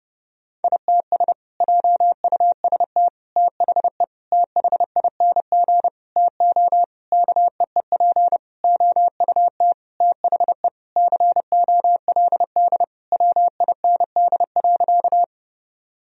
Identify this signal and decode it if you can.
Morse code, heard as a keyed tone, about 29 wpm, 700 Hz. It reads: ITS JUST THE THING TO KEEP OUT THE COLD WIND.
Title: Sentences from Top 1000 Words - Review Part 2 - 30wpm